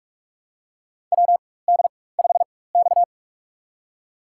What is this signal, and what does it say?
Morse code, heard as a keyed tone, about 46 wpm, spaced as for 21 wpm, 700 Hz. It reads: WD5X
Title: USA Call Signs - 4 Characters - 4x Character Spacing - 45wpm